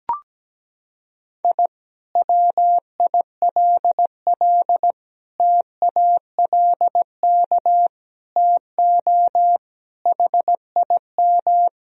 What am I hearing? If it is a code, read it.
Morse code, heard as a keyed tone, about 17 wpm, 700 Hz. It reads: I WILL TALK TO HIM
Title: Sentences from Top 1000 Words - Review Part 2 - 17wpm